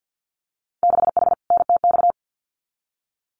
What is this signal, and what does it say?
Morse code, heard as a keyed tone, about 50 wpm, 700 Hz. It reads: 6H NTX
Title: Winter Field Day Exchanges Encore 50wpm